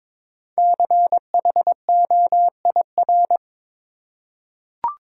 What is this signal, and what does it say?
Morse code, heard as a keyed tone, about 22 wpm, 700 Hz. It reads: CHOIR
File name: 5 Letter Words - First 3 Letters Given - 22wpm